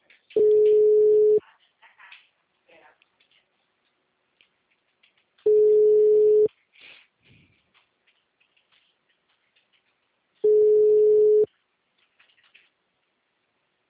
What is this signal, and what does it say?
Ringback tone (Europe, 425 Hz).
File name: out- -2007- - - .4